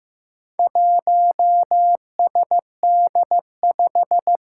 Morse code, heard as a keyed tone, about 15 wpm, 700 Hz. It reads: 1SD5
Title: Four Challenging Letter-Number Combinations 15wpm